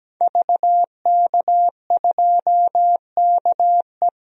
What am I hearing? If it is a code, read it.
Morse code, heard as a keyed tone, about 17 wpm, 700 Hz. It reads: VK2KE
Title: DXCC-100 Call Signs - 5 Characters - 17wpm